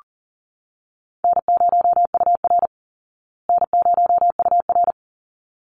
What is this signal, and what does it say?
Morse code, heard as a keyed tone, about 40 wpm, 700 Hz. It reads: N0UR N0UR
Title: Call Signs Repeated 3x - 40wpm